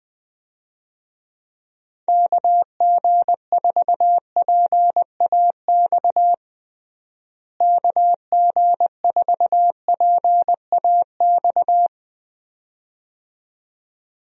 Morse code, heard as a keyed tone, about 21 wpm, 700 Hz. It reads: KG4PAX KG4PAX
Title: USA Call Signs - 6 Characters - Repeated 2x - 20wpm